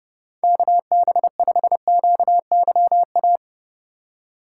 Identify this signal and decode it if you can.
Morse code, heard as a keyed tone, about 32 wpm, 700 Hz. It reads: KB5QYA
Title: USA Call Signs - 2x Character Spacing - 30wpm